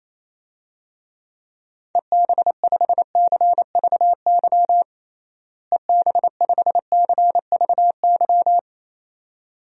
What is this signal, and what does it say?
Morse code, heard as a keyed tone, about 28 wpm, 700 Hz. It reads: EB5CVY EB5CVY
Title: DXCC-100 Call Signs - 6 Characters - Repeated 2x - 28wpm